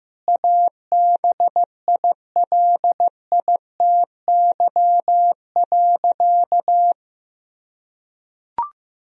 Morse code, heard as a keyed tone, about 16 wpm, 700 Hz. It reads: ABILITY.
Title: Sentences from Top 1000 Words - Review Part 3 - 15wpm